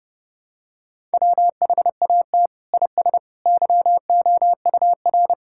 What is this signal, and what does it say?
Morse code, heard as a keyed tone, about 30 wpm, 700 Hz. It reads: WHAT IS YOUR
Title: Sentences from Top 1000 Words - Review Part 5 - 30wpm